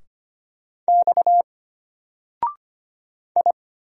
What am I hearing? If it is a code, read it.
Morse code, heard as a keyed tone, about 25 wpm, 700 Hz. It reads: X I
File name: Single-Letter-Number-25wpm